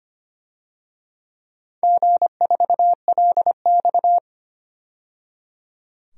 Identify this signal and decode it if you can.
Morse code, heard as a keyed tone, about 25 wpm, 700 Hz. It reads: G4LX
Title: DXCC-100 Call Signs - 4 Characters - 25wpm